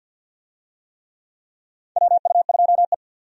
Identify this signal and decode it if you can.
Morse code, heard as a keyed tone, about 53 wpm, 700 Hz. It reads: WU2E
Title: USA Call Signs - 4 Characters - 50wpm